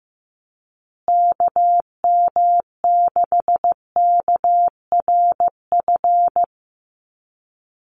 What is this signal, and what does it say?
Morse code, heard as a keyed tone, about 15 wpm, 700 Hz. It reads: KM6KRF
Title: USA Call Signs Repeated 3x 15wpm